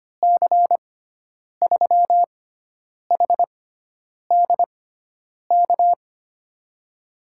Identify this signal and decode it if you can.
Morse code, heard as a keyed tone, about 25 wpm, spaced as for 9 wpm, 700 Hz. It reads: C3HDK
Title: USA Call Signs - 6 Characters - 6x Character Spacing - 25wpm